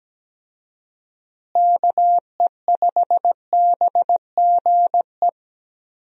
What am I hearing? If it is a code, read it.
Morse code, heard as a keyed tone, about 17 wpm, 700 Hz. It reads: KE5BGE
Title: USA Call Signs - 6 Characters - Repeated 2x - 17wpm